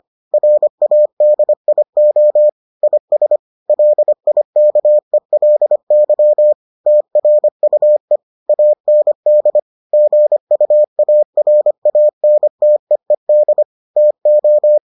Morse code, heard as a keyed tone, about 25 wpm, 600 Hz. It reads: RADIO IS LIKELY TRUE AND GUARANTEED TO